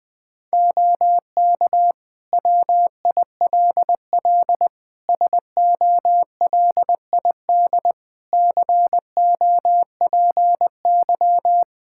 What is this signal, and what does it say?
Morse code, heard as a keyed tone, about 21 wpm, 700 Hz. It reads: OK WILL SOLID COPY
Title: Practice QSO - Exchange 3 - More Info - 20wpm